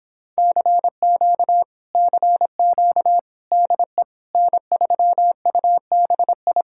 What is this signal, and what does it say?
Morse code, heard as a keyed tone, about 26 wpm, 700 Hz. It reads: CQ CQ DE N3UBI